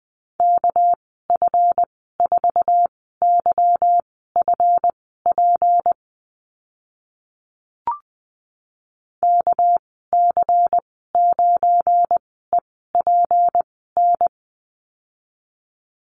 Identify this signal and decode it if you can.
Morse code, heard as a keyed tone, about 20 wpm, spaced as for 14 wpm, 700 Hz. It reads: KF4YFP KC9EPN
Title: 213 Lesson 34g - Number 8 - Call Signs 20wpm